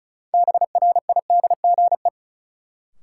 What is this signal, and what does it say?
Morse code, heard as a keyed tone, about 36 wpm, 700 Hz. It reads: BRIDGE
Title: Sentences from Top 1200 Words - New Word Drill Alt - 35wpm